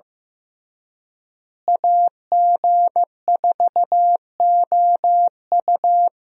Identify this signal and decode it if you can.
Morse code, heard as a keyed tone, about 15 wpm, 700 Hz. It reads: AG4OU
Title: USA Call Signs - 5 Characters - Repeated 3x - 15wpm